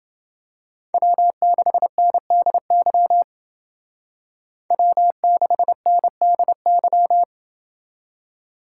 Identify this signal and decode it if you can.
Morse code, heard as a keyed tone, about 29 wpm, 700 Hz. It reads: W6NDY W6NDY